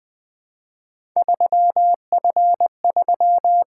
Morse code, heard as a keyed tone, about 20 wpm, 700 Hz. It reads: 3F3
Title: Three Letter-Number Combinations - 20wpm